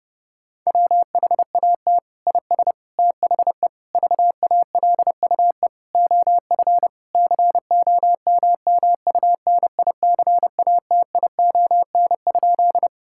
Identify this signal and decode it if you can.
Morse code, heard as a keyed tone, about 30 wpm, 700 Hz. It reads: WHAT IS THE VALUE OF COMMUNICATION?
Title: Sentences from Top 1000 Words - Review Part 6 - 30wpm